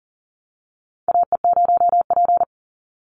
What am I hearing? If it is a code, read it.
Morse code, heard as a keyed tone, about 39 wpm, 700 Hz. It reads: AE0P
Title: Call Signs - 40wpm